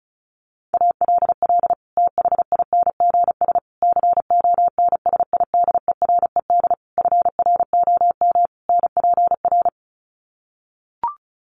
Morse code, heard as a keyed tone, about 34 wpm, 700 Hz. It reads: ALL THINGS CONSIDERED FROM NPR
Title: Sets of 5 Words Encore 35wpm